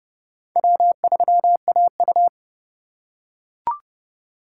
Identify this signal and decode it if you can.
Morse code, heard as a keyed tone, about 30 wpm, 700 Hz. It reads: W3AU
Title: USA Call Signs - 4 Characters - 3x Character Spacing - 30wpm